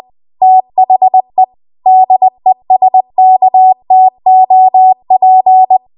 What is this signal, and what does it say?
Morse code, heard as a keyed tone, about 20 wpm, 750 Hz. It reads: THE DESKTOP
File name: Archive / 20 WPM / 20WPM